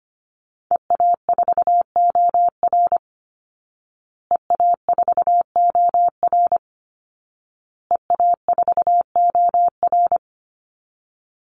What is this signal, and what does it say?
Morse code, heard as a keyed tone, about 25 wpm, 700 Hz. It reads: EA4OR EA4OR EA4OR
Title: Call Signs Repeated 2x - 25wpm